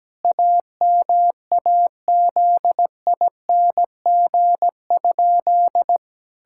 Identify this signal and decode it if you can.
Morse code, heard as a keyed tone, about 17 wpm, 700 Hz. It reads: AMAZING?
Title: Pangram Challenge v2 - 17wpm